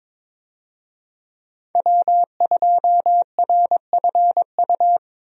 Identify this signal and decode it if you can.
Morse code, heard as a keyed tone, about 22 wpm, 700 Hz. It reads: W2RFU
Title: USA Call Signs - 5 Characters - 22wpm